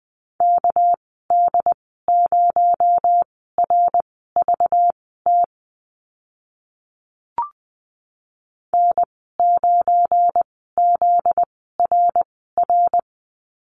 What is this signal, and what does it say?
Morse code, heard as a keyed tone, about 20 wpm, spaced as for 14 wpm, 700 Hz. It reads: KD0RVT N9ZRR
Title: 245 Lesson 39g - Letter z - Call Signs 20wpm